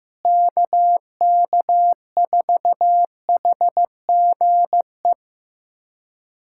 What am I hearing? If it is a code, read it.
Morse code, heard as a keyed tone, about 15 wpm, 700 Hz. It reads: KK4HGE
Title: USA Callsign DE Callsign - 6 Characters - 15wpm